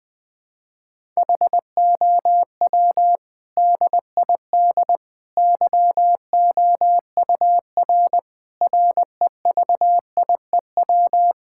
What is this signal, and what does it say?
Morse code, heard as a keyed tone, about 20 wpm, 700 Hz. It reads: HOW DID YOUR REVIEW
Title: Sentences from Top 1000 Words - Review Part 6 - 20wpm